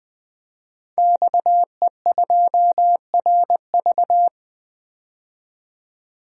Morse code, heard as a keyed tone, about 20 wpm, 700 Hz. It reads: XE2RV
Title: DXCC-100 Call Signs - 5 Characters - 20wpm